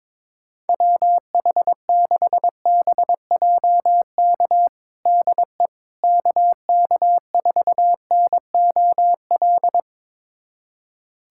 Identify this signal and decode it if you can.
Morse code, heard as a keyed tone, about 22 wpm, 700 Hz. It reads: WH6BJK DE KK4NOL
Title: USA Callsign DE Callsign 22wpm